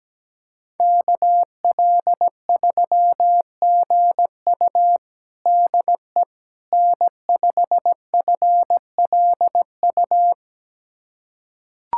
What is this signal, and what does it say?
Morse code, heard as a keyed tone, about 17 wpm, 700 Hz. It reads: KL3GU DE N5FLU
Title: USA Callsign DE Callsign - 5 Characters - 17wpm